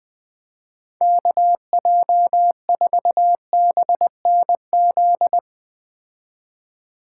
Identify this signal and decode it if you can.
Morse code, heard as a keyed tone, about 21 wpm, 700 Hz. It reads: KJ4BNZ